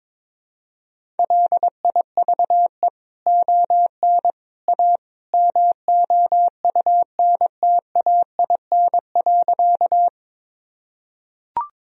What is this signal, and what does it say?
Morse code, heard as a keyed tone, about 21 wpm, 700 Hz. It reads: LIVE ON A MOUNTAIN.
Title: Sentences from Top 1000 Words - Review Part 1 - 22wpm